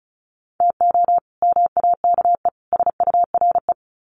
Morse code, heard as a keyed tone, about 34 wpm, 700 Hz. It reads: TO MAKE SURE
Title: Sets of 3 Words 35wpm